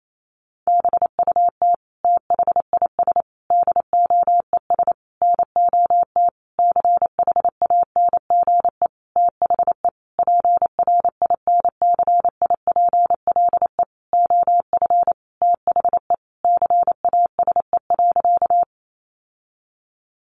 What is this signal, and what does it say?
Morse code, heard as a keyed tone, about 28 wpm, 700 Hz. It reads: BUT THIS DOES NOT CHANGE THE PRINCIPLE OF THE CASE.
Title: Sentences from Top 1400 Words 28wpm